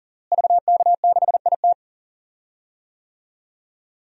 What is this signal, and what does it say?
Morse code, heard as a keyed tone, about 40 wpm, 700 Hz. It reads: VK6IT
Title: DXCC-100 Call Signs - 5 Characters - Repeated 3x - 40wpm